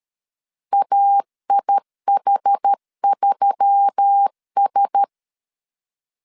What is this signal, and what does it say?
Morse code, heard as a keyed tone, about 13 wpm, 800 Hz. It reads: AIH3S
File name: CW / DET / 1 6 1500C S A 12wpm 3440